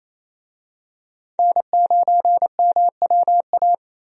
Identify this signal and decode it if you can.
Morse code, heard as a keyed tone, about 28 wpm, 700 Hz. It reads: N9MWA
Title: USA Call Signs - 5 Characters - Repeated 3x - 28wpm